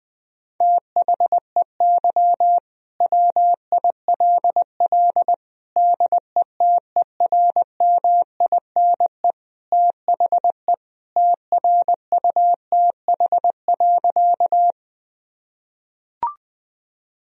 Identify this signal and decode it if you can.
Morse code, heard as a keyed tone, about 20 wpm, 700 Hz. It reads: THEY WILL DETERMINE THE TRUTH.
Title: Sentences from Top 1000 Words - Review Part 4 - 20wpm